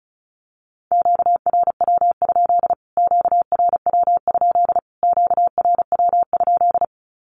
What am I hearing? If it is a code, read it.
Morse code, heard as a keyed tone, about 34 wpm, 700 Hz. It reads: QRW? QRW? QRW?
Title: Q-Codes Repeated 3x 35wpm